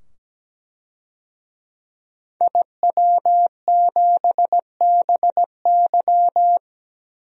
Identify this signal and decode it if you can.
Morse code, heard as a keyed tone, about 17 wpm, 700 Hz. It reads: IW7BY
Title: DXCC-100 Call Signs - 5 Characters - 17wpm